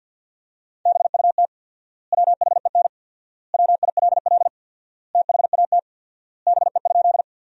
Morse code, heard as a keyed tone, about 50 wpm, 700 Hz. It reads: BUT WHEN WILL THAT BE?
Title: 4x Word-Spacing - Sentences from Top 100 Words - 50wpm